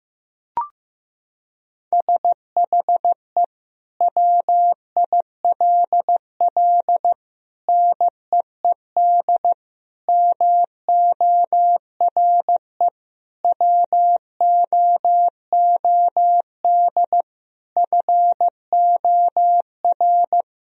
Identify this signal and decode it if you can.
Morse code, heard as a keyed tone, about 15 wpm, 700 Hz. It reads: SHE WILL NEED MORE WOOD FOR
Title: Sentences from Top 1000 Words - Review Part 2 - 15wpm